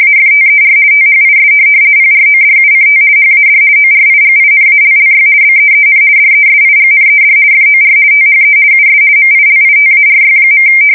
RTTY, about 50 baud. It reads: AFTER THIS PRODUCT HAD ORIGINALLY BEEN DESIGNED FOR THE US MILITARY.
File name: fsk170-50B / teletype-corp